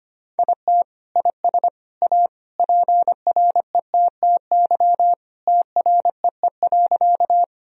Morse code, heard as a keyed tone, about 25 wpm, 700 Hz. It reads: IT IS A PRETTY TREE.
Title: Sentences from Top 1000 Words - Review Part 1 - 25wpm